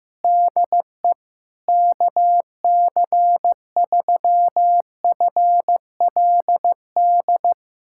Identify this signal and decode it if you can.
Morse code, heard as a keyed tone, about 15 wpm, 700 Hz. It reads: DE KC3FLD
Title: USA Callsign DE Callsign - 6 Characters - 15wpm